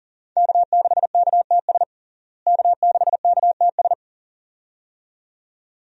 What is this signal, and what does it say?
Morse code, heard as a keyed tone, about 40 wpm, 700 Hz. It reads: K6KTS K6KTS
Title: USA Call Signs - 5 Characters - Repeated 2x - 40wpm